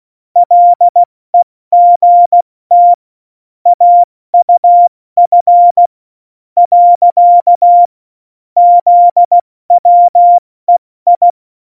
Morse code, heard as a keyed tone, about 16 wpm, 700 Hz. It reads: LEGT AUF . ZWEI
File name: Morsekurs / Witze / Witz 02. Oktober 2018, 70 F80